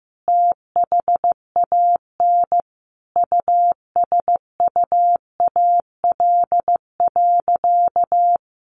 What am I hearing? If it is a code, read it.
Morse code, heard as a keyed tone, about 15 wpm, 700 Hz. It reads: THAN USUAL.
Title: Sentences from Top 1400 Words 15wpm